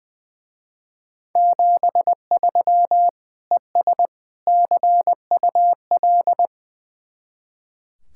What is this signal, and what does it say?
Morse code, heard as a keyed tone, about 22 wpm, 700 Hz. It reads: 73 ES CUL
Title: Speed-Racing Common QSO Phrases - 20wpm